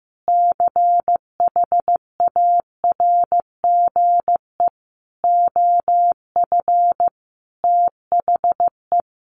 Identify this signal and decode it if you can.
Morse code, heard as a keyed tone, about 15 wpm, 700 Hz. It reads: CHARGE OF THE
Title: Sentences from Top 900 Words 15wpm